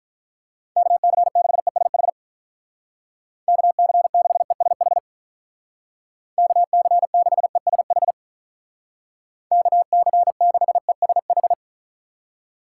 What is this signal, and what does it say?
Morse code, heard as a keyed tone, about 43 wpm, 700 Hz. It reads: KC6ESH KC6ESH KC6ESH KC6ESH
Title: Speed Racing - USA Call Signs 35wpm